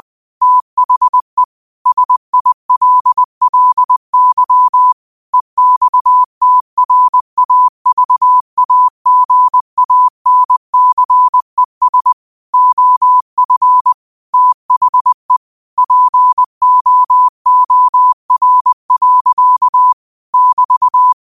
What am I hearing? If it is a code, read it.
Morse code, heard as a keyed tone, about 20 wpm, 1000 Hz. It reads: THE SILLY EXTRAVAGANCES OF THE POOR. <BT>
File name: qotdfiles / 20wpm